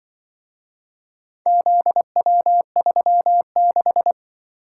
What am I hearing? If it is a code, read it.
Morse code, heard as a keyed tone, about 23 wpm, 700 Hz. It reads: ZW36
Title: Speed-Racing - Four Letter-Number Combinations - 20wpm